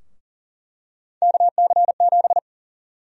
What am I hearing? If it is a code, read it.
Morse code, heard as a keyed tone, about 40 wpm, 700 Hz. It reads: KC7
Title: USA Call Sign Prefixes - Repeated 2x - 40wpm